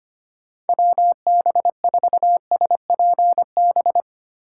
Morse code, heard as a keyed tone, about 25 wpm, 700 Hz. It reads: WB4SPB
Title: K1USN SST QSO Practice - 25wpm